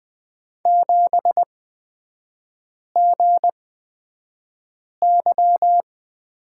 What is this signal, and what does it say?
Morse code, heard as a keyed tone, about 20 wpm, spaced as for 5 wpm, 700 Hz. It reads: 7GY
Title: 229 Lesson 37e - Letter x - Three Characters 20wpm